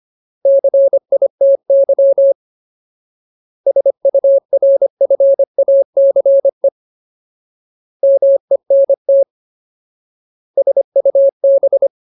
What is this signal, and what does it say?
Morse code, heard as a keyed tone, about 25 wpm, 550 Hz. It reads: CITY SURFACE MENT SUB
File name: Ultimate Word List / 25WPM Ultimate